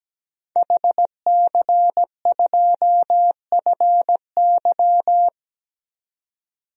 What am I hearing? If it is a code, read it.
Morse code, heard as a keyed tone, about 17 wpm, 700 Hz. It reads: HC2FY
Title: DXCC-100 Call Signs - 5 Characters - Repeated 2x - 17wpm